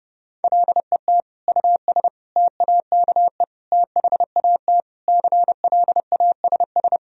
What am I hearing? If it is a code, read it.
Morse code, heard as a keyed tone, about 30 wpm, 700 Hz. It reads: LET US TAKE THAT CLASS
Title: Sentences from Top 1000 Words - Review Part 2 - 30wpm